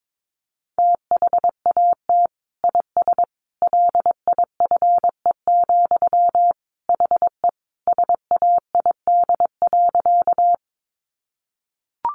Morse code, heard as a keyed tone, about 22 wpm, 700 Hz. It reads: THAT IS LIFE, HE SAID.
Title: Sentences from Top 900 Words 22wpm